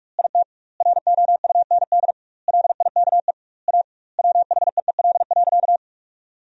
Morse code, heard as a keyed tone, about 46 wpm, 700 Hz. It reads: IT ROUND LIKE A WHEEL.
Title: 2x Word-Spacing - Sentences from Top 500 Words - 45wpm